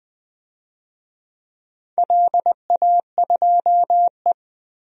Morse code, heard as a keyed tone, about 20 wpm, 700 Hz. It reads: LA2E
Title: DXCC-100 Call Signs - 4 Characters - Repeated 3x - 20wpm